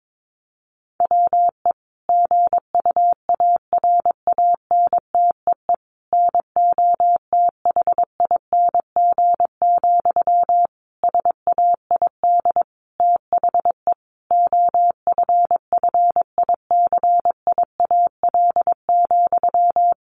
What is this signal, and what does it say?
Morse code, heard as a keyed tone, about 22 wpm, 700 Hz. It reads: WE GUARANTEE NOTHING, SAID THE OFFICIAL,
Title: A Sound Of Thunder 22wpm